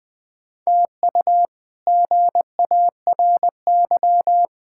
Morse code, heard as a keyed tone, about 21 wpm, 700 Hz. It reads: TU GARY
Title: K1USN SST QSO Practice - 20wpm